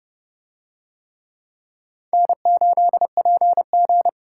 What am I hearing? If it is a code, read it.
Morse code, heard as a keyed tone, about 30 wpm, 700 Hz. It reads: N8PG